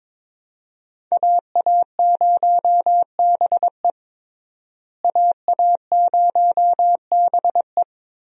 Morse code, heard as a keyed tone, about 22 wpm, 700 Hz. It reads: AA0BE AA0BE
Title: USA Call Signs - 5 Characters - Repeated 3x - 22wpm